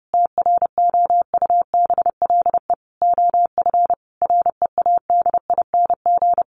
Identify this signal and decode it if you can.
Morse code, heard as a keyed tone, about 30 wpm, 700 Hz. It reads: TROUBLE OF READING
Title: Sentences from Top 1200 Words 30wpm